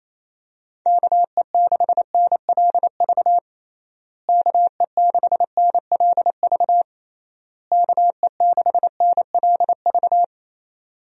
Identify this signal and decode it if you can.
Morse code, heard as a keyed tone, about 28 wpm, 700 Hz. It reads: KE6NLV KE6NLV KE6NLV
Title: USA Call Signs - 6 Characters - Repeated 3x - 28wpm